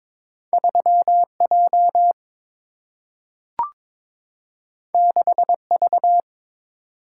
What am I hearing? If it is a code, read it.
Morse code, heard as a keyed tone, about 22 wpm, 700 Hz. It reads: 3J 6V
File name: practice / 22wpm / Two Letter-Number Combinations - 22wpm